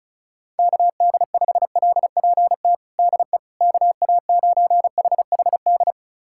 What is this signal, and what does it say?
Morse code, heard as a keyed tone, about 34 wpm, 700 Hz. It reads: KD5LPT DE KA9HHD
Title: USA Callsign DE Callsign 35wpm